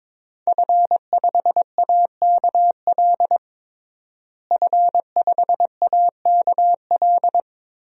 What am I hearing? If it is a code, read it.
Morse code, heard as a keyed tone, about 22 wpm, 700 Hz. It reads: F5AKL F5AKL
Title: DXCC-100 Call Signs - 5 Characters - Repeated 2x - 22wpm